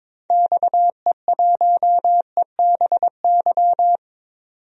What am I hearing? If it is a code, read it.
Morse code, heard as a keyed tone, about 21 wpm, 700 Hz. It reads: XE1EBY
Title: DXCC-100 Call Signs - 6 Characters - 22wpm